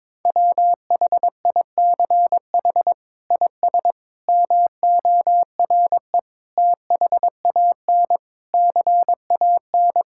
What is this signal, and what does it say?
Morse code, heard as a keyed tone, about 22 wpm, 700 Hz. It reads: WHICH IS MORE THAN CAN
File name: practice / 22wpm / Sets of 8 Words - 22wpm